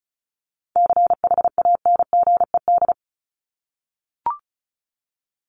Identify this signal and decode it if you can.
Morse code, heard as a keyed tone, about 34 wpm, 700 Hz. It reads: CHANGED
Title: ED Suffix - Top 100 Words with Base Words 35wpm